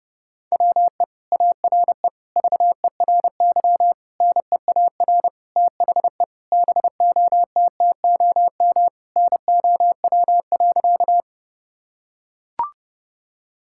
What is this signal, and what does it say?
Morse code, heard as a keyed tone, about 32 wpm, 700 Hz. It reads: WE ARE VERY NEAR THE BOTTOM NOW.
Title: Sentences from Top 1000 Words - Review Part 6 - 30wpm